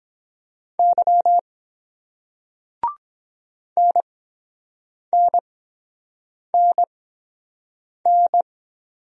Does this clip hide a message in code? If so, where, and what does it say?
Yes: Morse code from 0.8 to 8.4 s, heard as a keyed tone, about 23 wpm, spaced as for 7 wpm, 700 Hz. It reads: Y NNNN